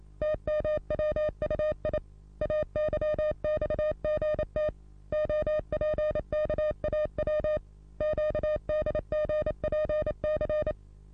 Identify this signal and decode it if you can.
Morse code, heard as a keyed tone, about 29 wpm, 610 Hz. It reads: TMWUI AYXGT OPKAW QDGPC
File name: repos / Morsekurs / DJ4UF Morsen27